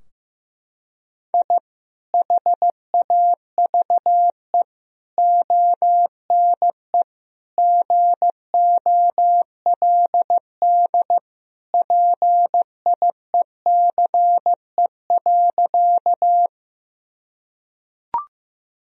Morse code, heard as a keyed tone, about 15 wpm, 700 Hz. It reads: I HAVE ONE GOLD PIECE.
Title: Sentences from Top 1000 Words - Review Part 3 - 15wpm